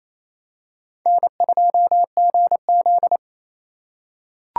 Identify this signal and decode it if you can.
Morse code, heard as a keyed tone, about 28 wpm, 700 Hz. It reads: N2GZ
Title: USA Call Signs - 4 Characters - 2x Character Spacing - 28wpm